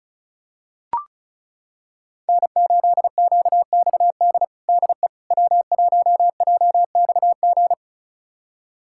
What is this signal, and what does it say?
Morse code, heard as a keyed tone, about 34 wpm, 700 Hz. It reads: N8QXD DE W1JXG
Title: USA Callsign DE Callsign 35wpm